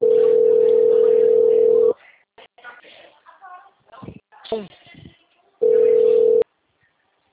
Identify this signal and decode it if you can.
Ringback tone (North America), 440 + 480 Hz, with speech.
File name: out- -2009- - - .13644